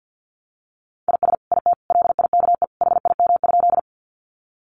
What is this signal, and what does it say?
Morse code, heard as a keyed tone, about 50 wpm, 700 Hz. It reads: IS IT LIKE HIM?
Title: Sentences from Top 100 Words 50wpm